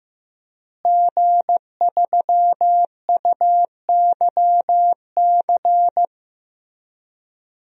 Morse code, heard as a keyed tone, about 15 wpm, 700 Hz. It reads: G3UYC
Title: DXCC-100 Call Signs - 5 Characters - Repeated 3x - 15wpm